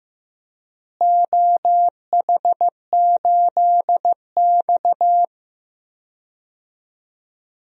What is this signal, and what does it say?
Morse code, heard as a keyed tone, about 16 wpm, 700 Hz. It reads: OH8X
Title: DXCC-100 Call Signs - 4 Characters - Repeated 2x - 15wpm